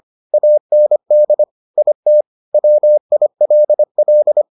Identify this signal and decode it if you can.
Morse code, heard as a keyed tone, about 25 wpm, 600 Hz. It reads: AND IT WILL